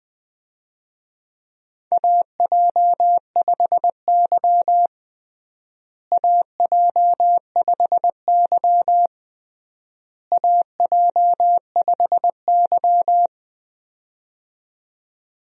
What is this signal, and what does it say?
Morse code, heard as a keyed tone, about 20 wpm, 700 Hz. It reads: AJ5Y AJ5Y AJ5Y
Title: USA Call Signs - 4 Characters - Repeated 3x - 20wpm